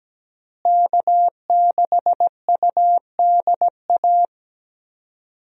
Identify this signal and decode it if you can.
Morse code, heard as a keyed tone, about 17 wpm, 700 Hz. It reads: K6UDA
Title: USA Call Signs - 5 Characters - Repeated 2x - 17wpm